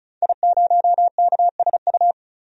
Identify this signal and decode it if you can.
Morse code, heard as a keyed tone, about 34 wpm, 700 Hz. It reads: I0KSU
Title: DXCC-100 Call Signs - 5 Characters - 35wpm